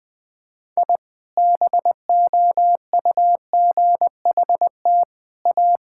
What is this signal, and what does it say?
Morse code, heard as a keyed tone, about 20 wpm, 700 Hz. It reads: I BOUGHT A
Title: Sentences with 3-Word Phrases - Spoken Beginning - 20wpm